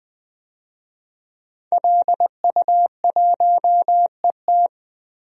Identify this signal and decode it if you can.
Morse code, heard as a keyed tone, about 20 wpm, 700 Hz. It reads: LU1ET
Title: DXCC-100 Call Signs - 5 Characters - 20wpm